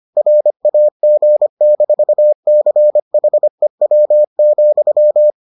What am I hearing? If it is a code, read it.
Morse code, heard as a keyed tone, about 25 wpm, 600 Hz. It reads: RAG-CHEW,